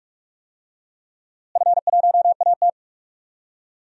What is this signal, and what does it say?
Morse code, heard as a keyed tone, about 43 wpm, 700 Hz. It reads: F1AT
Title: DXCC-100 Call Signs - 4 Characters - Repeated 3x - 45wpm